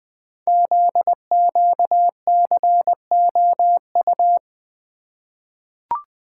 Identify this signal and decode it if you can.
Morse code, heard as a keyed tone, about 20 wpm, 700 Hz. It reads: ZQCOU
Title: Code Groups - 5 Letters - 20wpm